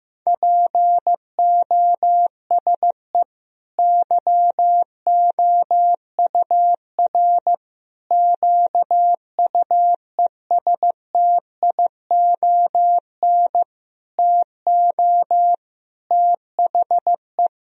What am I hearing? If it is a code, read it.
Morse code, heard as a keyed tone, about 16 wpm, 700 Hz. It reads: POSE YOUR QUESTION TO THE
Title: Sentences from Top 1000 Words - Review Part 2 - 15wpm